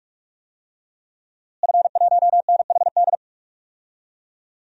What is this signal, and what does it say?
Morse code, heard as a keyed tone, about 46 wpm, 700 Hz. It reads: F1NHD
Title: DXCC-100 Call Signs - 5 Characters - Repeated 2x - 45wpm